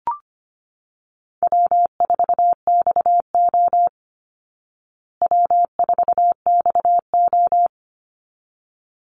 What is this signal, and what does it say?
Morse code, heard as a keyed tone, about 25 wpm, 700 Hz. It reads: W4XO W4XO
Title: Easy Call Signs Repeated 2x - 25wpm